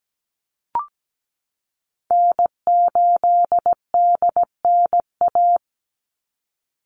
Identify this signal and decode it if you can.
Morse code, heard as a keyed tone, about 17 wpm, 700 Hz. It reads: N8DNA
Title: Call Signs Repeated 3x - 17wpm